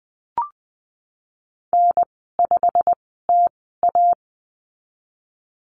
Morse code, heard as a keyed tone, about 20 wpm, spaced as for 14 wpm, 700 Hz. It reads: N5TA
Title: 128 Lesson 21g - Number 6 - Call Signs 20wpm